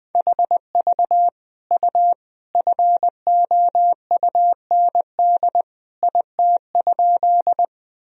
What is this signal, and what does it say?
Morse code, heard as a keyed tone, about 20 wpm, 700 Hz. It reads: HV U FOUND IT?
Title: Sentences from Top 300 Words and Abbreviations 20wpm